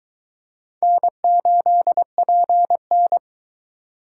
Morse code, heard as a keyed tone, about 21 wpm, 700 Hz. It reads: N8PN